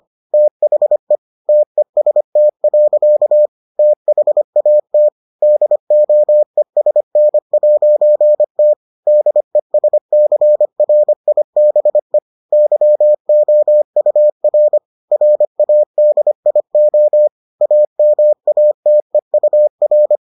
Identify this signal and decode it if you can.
Morse code, heard as a keyed tone, about 25 wpm, 600 Hz. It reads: THE TEST. THAT DOESN'T DESCRIBE YOUR RADIO AMATEUR